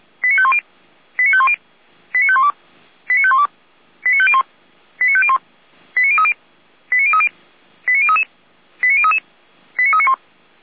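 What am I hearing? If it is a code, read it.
Five-tone selective call (ZVEI), heard as quick runs of five tones, tones of about 70 ms. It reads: 89510 89510 89512 89512 89501 89501 89030 89030 89033 89033 89391
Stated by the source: ZVEI five-tone call 89050 received by SDR.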